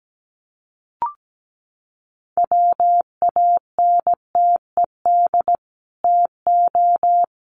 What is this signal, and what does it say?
Morse code, heard as a keyed tone, about 17 wpm, 700 Hz. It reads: WANTED TO
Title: Sets of 2 Words 17wpm